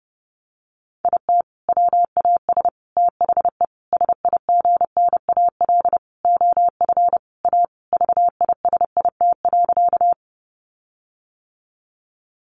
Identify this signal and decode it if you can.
Morse code, heard as a keyed tone, about 30 wpm, 700 Hz. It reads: IT WAS THE SIGNAL OF A VISIT.
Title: Sentences from Top 1400 Words 30wpm